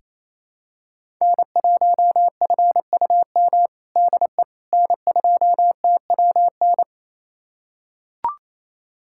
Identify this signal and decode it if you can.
Morse code, heard as a keyed tone, about 28 wpm, 700 Hz. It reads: N1FUM DE N2TWN
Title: USA Callsign DE Callsign - 5 Characters - 28wpm